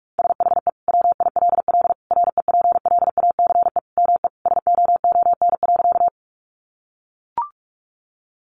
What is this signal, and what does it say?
Morse code, heard as a keyed tone, about 44 wpm, 700 Hz. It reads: SHE WILL REPLACE ME SOON.
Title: Sentences from Top 1400 Words 45wpm